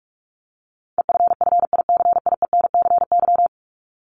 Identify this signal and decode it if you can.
Morse code, heard as a keyed tone, about 44 wpm, 700 Hz. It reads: EFFICIENCY
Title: Top 1501-1750 Words 45wpm